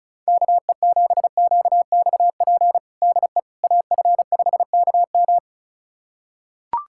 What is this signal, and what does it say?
Morse code, heard as a keyed tone, about 36 wpm, 700 Hz. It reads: KE7QXP DE AF5KM
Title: USA Callsign DE Callsign 35wpm